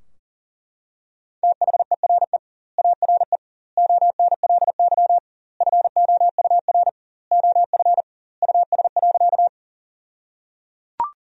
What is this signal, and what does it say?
Morse code, heard as a keyed tone, about 37 wpm, 700 Hz. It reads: THERE ARE ONLY FOUR OF US.
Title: 2x Word-Spacing - Sentences from Top 300 Words - 40wpm